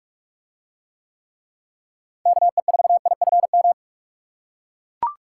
Morse code, heard as a keyed tone, about 41 wpm, 700 Hz. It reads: KE4IFM